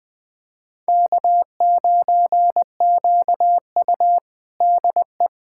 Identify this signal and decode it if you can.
Morse code, heard as a keyed tone, about 21 wpm, 700 Hz. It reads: K9QU DE
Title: USA Callsign DE Callsign - 4 Characters - 20wpm